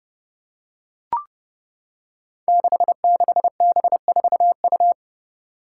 Morse code, heard as a keyed tone, about 30 wpm, 700 Hz. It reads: 66B4U
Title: Five Challenging Letter-Number Combinations 30wpm